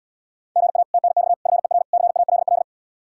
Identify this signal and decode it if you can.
Morse code, heard as a keyed tone, about 77 wpm, 700 Hz. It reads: GA TNX FER CALL